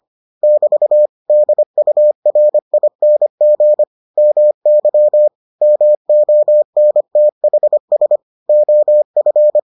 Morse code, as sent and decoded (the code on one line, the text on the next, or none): -...- / -.. ..- .-. .. -. --. / -- -.-- / -- --- -. - .... ... / --- ..-.
<BT> DURING MY MONTHS OF